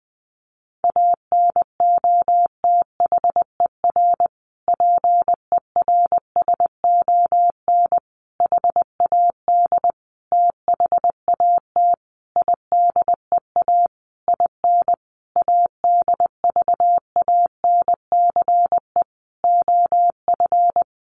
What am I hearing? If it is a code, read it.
Morse code, heard as a keyed tone, about 20 wpm, 700 Hz. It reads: ANOTHER PERSON HAD THAT IDEA IN ADVANCE OF
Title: Sentences from Top 1400 Words 20wpm